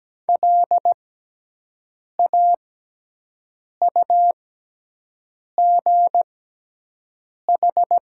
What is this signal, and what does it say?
Morse code, heard as a keyed tone, about 17 wpm, spaced as for 6 wpm, 700 Hz. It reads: LAUGH
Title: 6x Character Spacing - Top 500 Words - 17wpm